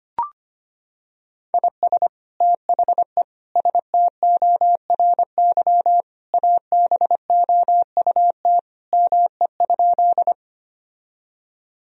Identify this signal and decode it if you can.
Morse code, heard as a keyed tone, about 25 wpm, 700 Hz. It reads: IS THE STORY ABOUT ME?
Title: Sentences from Top 1000 Words - Review Part 1 - 25wpm